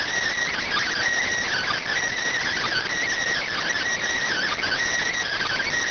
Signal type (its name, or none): Emergency Alert System (SAME) tones